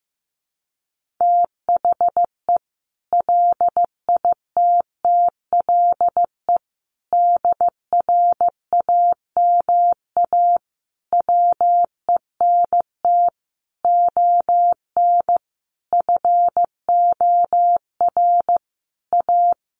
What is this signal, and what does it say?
Morse code, heard as a keyed tone, about 15 wpm, 700 Hz. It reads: THE LITTLE DRAMA WENT ON FOR A
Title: Sentences from Top 1500 Words 15wpm